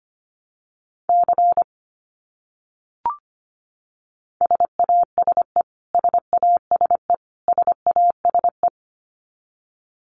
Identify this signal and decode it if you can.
Morse code, heard as a keyed tone, about 25 wpm, 700 Hz. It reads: C SASE SASE SASE
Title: Abbreviations Repeated 3x - 25wpm